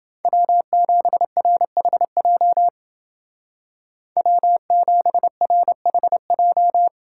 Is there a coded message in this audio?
Yes: Morse code from 0.2 to 6.9 s, heard as a keyed tone, about 29 wpm, 700 Hz. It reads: W7RHJ W7RHJ